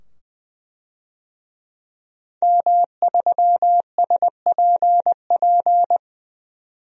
Morse code, heard as a keyed tone, about 21 wpm, 700 Hz. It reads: M3SPP
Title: DXCC-100 Call Signs - 5 Characters - 20wpm